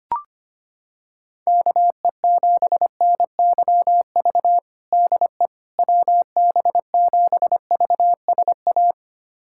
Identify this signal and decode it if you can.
Morse code, heard as a keyed tone, about 25 wpm, 700 Hz. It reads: KE7NYV DE WB7VSA
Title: USA Callsign DE Callsign 25wpm